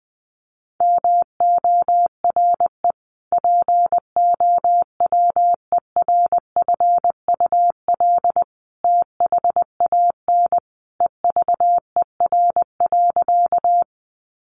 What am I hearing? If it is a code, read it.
Morse code, heard as a keyed tone, about 20 wpm, 700 Hz. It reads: MORE POWERFUL THAN EVER.
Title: Sentences from Top 1300 Words 20wpm